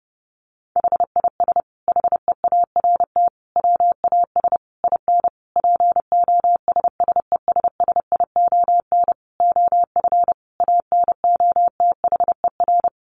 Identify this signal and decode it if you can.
Morse code, heard as a keyed tone, about 30 wpm, 700 Hz. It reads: HIS HEART WAS IN POSSESSION OF ANOTHER
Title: Sentences from Top 1400 Words 30wpm